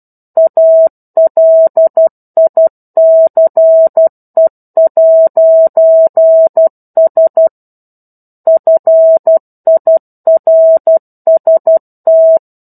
Morse code, heard as a keyed tone, about 12 wpm, 650 Hz. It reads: ALICE'S FIRST
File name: 01-Chapter-01 12 WPM